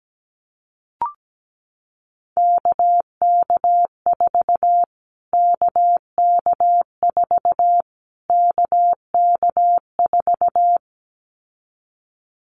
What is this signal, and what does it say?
Morse code, heard as a keyed tone, about 17 wpm, 700 Hz. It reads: KK4 KK4 KK4
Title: Callsign Prefixes for Top 25 DXCC - Repeated 3x - 17wpm